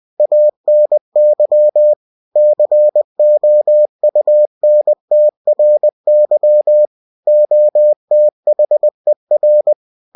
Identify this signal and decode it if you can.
Morse code, heard as a keyed tone, about 20 wpm, 600 Hz. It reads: ANY COUNTRY OTHER
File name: cw-books / winnie-the-pooh / winnie-20wpm-page-0114